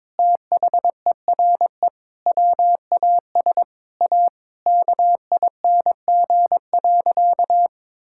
Morse code, heard as a keyed tone, about 22 wpm, 700 Hz. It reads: THERE WAS A KING.
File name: Sentences with 7-Word Phrases - 22wpm